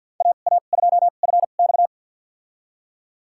Morse code, heard as a keyed tone, about 51 wpm, spaced as for 36 wpm, 700 Hz. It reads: AA2FX
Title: USA Call Signs - 5 Characters - 2x Character Spacing - 50wpm